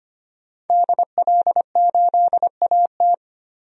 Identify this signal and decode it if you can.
Morse code, heard as a keyed tone, about 25 wpm, 700 Hz. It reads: DL8AT
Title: DXCC-100 Call Signs - 5 Characters - Repeated 3x - 25wpm